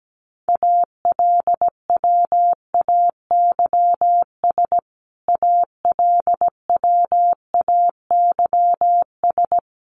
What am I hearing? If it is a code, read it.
Morse code, heard as a keyed tone, about 17 wpm, 700 Hz. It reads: ALWAYS ALWAYS
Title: Top 500 Words Repeated 17wpm